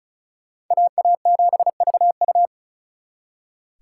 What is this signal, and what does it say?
Morse code, heard as a keyed tone, about 46 wpm, 700 Hz. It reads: AA7VU